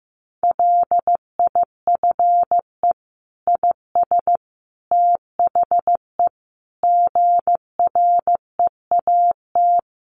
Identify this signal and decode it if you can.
Morse code, heard as a keyed tone, about 15 wpm, 700 Hz. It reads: LIFE IS THE GREAT
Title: Sentences from Top 900 Words 15wpm